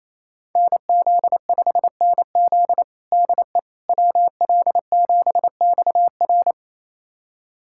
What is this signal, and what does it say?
Morse code, heard as a keyed tone, about 27 wpm, 700 Hz. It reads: NZ5NZ DE WL7XR
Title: USA Callsign DE Callsign - 5 Characters - 28wpm